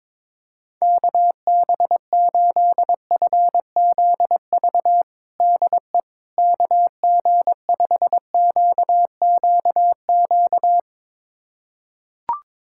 Morse code, heard as a keyed tone, about 22 wpm, 700 Hz. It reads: KB8FZV DE KG5QQQ
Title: USA Callsign DE Callsign - 6 Characters - 22wpm